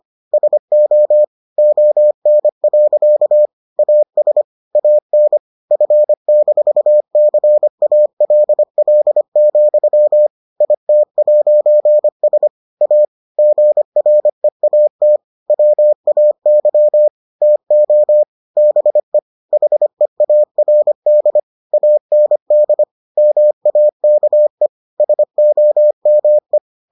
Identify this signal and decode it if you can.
Morse code, heard as a keyed tone, about 25 wpm, 600 Hz. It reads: SO ON. AS AN F-CALL, IT'S A GREAT WAY TO BE HEARD AND MAKE SOME